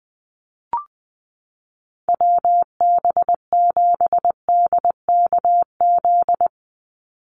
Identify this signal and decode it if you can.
Morse code, heard as a keyed tone, about 19 wpm, 700 Hz. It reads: WB7DKZ
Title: Call Signs Repeated 3x - 20wpm